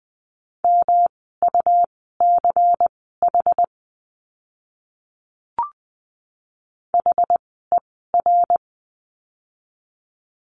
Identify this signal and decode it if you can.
Morse code, heard as a keyed tone, about 20 wpm, spaced as for 15 wpm, 700 Hz. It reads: MUCH HER
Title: 146 Lesson 24f - Letter y - Words 20wpm